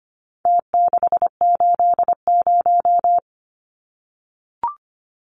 Morse code, heard as a keyed tone, about 25 wpm, 700 Hz. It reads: T680
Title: Numbers - Contest Style - 4-Digit 25wpm